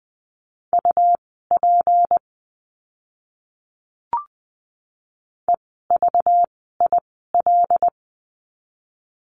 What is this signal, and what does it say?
Morse code, heard as a keyed tone, about 20 wpm, spaced as for 15 wpm, 700 Hz. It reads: UP EVIL
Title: 192 Lesson 31f - Letter v - Words 20wpm